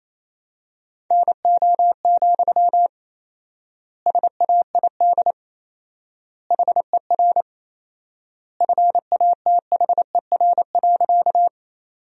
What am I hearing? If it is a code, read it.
Morse code, heard as a keyed tone, about 27 wpm, 700 Hz. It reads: NO, SAID HER FATHER.
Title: 4x Word-Spacing - Sentences from Top 300 Words - 28wpm